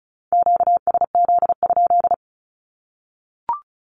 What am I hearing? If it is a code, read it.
Morse code, heard as a keyed tone, about 34 wpm, 700 Hz. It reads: QSZ?
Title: Q-Codes Repeated 3x 35wpm